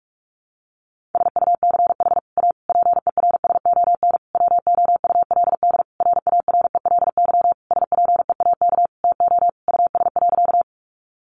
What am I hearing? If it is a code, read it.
Morse code, heard as a keyed tone, about 44 wpm, 700 Hz. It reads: SUCH A PERSON WOULD RARELY SPEAK TO US.
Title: Sentences from Top 1400 Words 45wpm